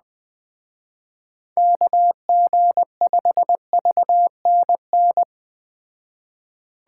Morse code, heard as a keyed tone, about 20 wpm, 700 Hz. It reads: KG5VNN
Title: Speed Racing - USA 2x3 Call Signs 17wpm